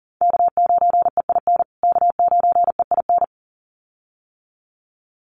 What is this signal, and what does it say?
Morse code, heard as a keyed tone, about 40 wpm, 700 Hz. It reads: K9EIN K9EIN
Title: USA Call Signs Repeated 2x 40wpm